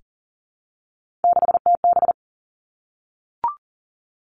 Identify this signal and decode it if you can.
Morse code, heard as a keyed tone, about 38 wpm, 700 Hz. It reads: 6TB
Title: Three Challenging Letter-Number Combinations 40wpm